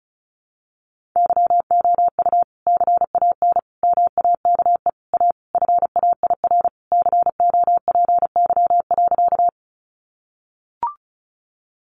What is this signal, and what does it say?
Morse code, heard as a keyed tone, about 34 wpm, 700 Hz. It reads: YOU CAN MAKE A FAIR COPY.
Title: Sentences from Top 1400 Words 35wpm